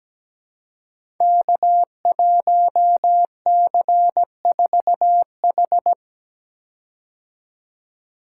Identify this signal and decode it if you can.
Morse code, heard as a keyed tone, about 17 wpm, 700 Hz. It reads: K1C4H
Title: Code Groups - 5 Letter:Numbers - 17wpm